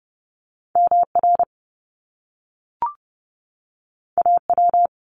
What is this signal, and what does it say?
Morse code, heard as a keyed tone, about 30 wpm, 700 Hz. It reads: MR AW
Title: Top Two Letter Combinations in Words 30wpm